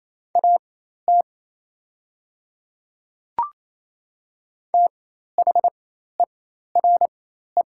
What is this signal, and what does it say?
Morse code, heard as a keyed tone, about 28 wpm, spaced as for 13 wpm, 700 Hz. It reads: AT THERE
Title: 4x Character Spacing - Top 100 Words - 28wpm